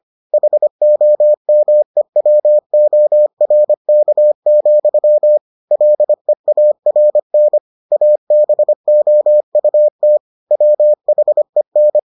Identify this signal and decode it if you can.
Morse code, heard as a keyed tone, about 25 wpm, 600 Hz. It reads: HOMEWORK, LEARN ABOUT WHEN